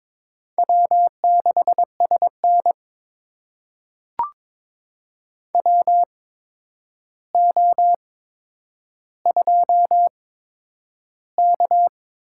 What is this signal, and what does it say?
Morse code, heard as a keyed tone, about 21 wpm, spaced as for 9 wpm, 700 Hz. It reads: W6SN WO2K
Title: USA Call Signs - 4 Characters - 8x Character Spacing - 22wpm